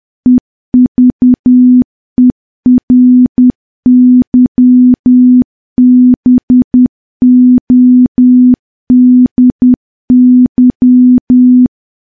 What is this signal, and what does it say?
Morse code, heard as a keyed tone, about 10 wpm, 250 Hz. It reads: EVERYBODY
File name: morse / morse-training-10WPM